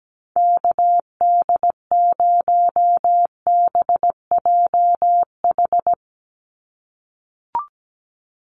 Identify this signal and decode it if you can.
Morse code, heard as a keyed tone, about 17 wpm, 700 Hz. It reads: KD0BJH
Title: USA Call Signs 17wpm